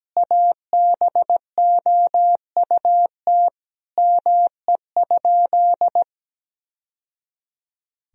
Morse code, heard as a keyed tone, about 17 wpm, 700 Hz. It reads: ABOUT ME?
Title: Sentences from Top 1000 Words - Review Part 1 - 17wpm